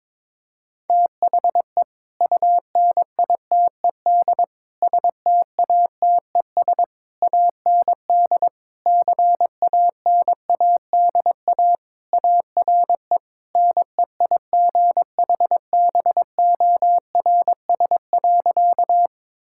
Morse code, heard as a keyed tone, about 22 wpm, 700 Hz. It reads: THE UNITED STATES AND CANADA ARE NEIGHBORS.
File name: practice / 22wpm / Sentences with 3-Word Phrases - 22wpm